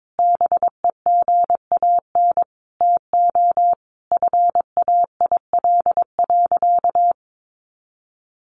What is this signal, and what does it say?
Morse code, heard as a keyed tone, about 22 wpm, 700 Hz. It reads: BEGAN TO FAIL.
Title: Sentences from Top 1100 Words 22wpm